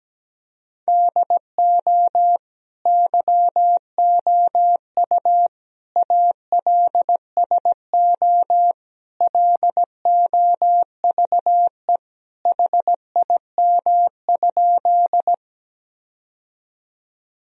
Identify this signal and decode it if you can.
Morse code, heard as a keyed tone, about 17 wpm, 700 Hz. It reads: DO YOU ALSO LOVE HIM?
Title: Sentences from Top 1000 Words - Review Part 1 - 17wpm